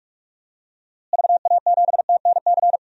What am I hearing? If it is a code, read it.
Morse code, heard as a keyed tone, about 46 wpm, 700 Hz. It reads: VA7TNC
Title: DXCC-100 Call Signs - 6 Characters - Repeated 3x - 45wpm